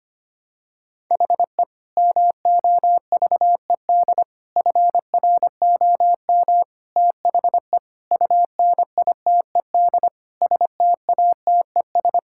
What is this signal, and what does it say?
Morse code, heard as a keyed tone, about 25 wpm, 700 Hz. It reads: HE MOVED FROM THE UNITED STATES
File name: practice / 25wpm / Sentences with 4-Word Phrases - 25wpm